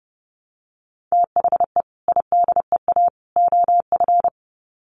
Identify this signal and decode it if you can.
Morse code, heard as a keyed tone, about 30 wpm, 700 Hz. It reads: THE IDEA OF
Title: Sets of 3 Words 30wpm